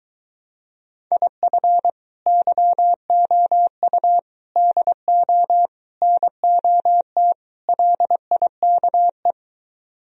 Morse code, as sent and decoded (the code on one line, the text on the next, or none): .. ..-. / -.-- --- ..- / -.. --- / -. --- - / .-.. .. -.- .
IF YOU DO NOT LIKE